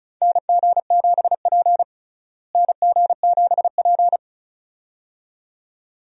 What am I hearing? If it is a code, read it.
Morse code, heard as a keyed tone, about 36 wpm, 700 Hz. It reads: NG7P NG7P
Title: USA Call Signs - 4 Characters - Repeated 2x - 35wpm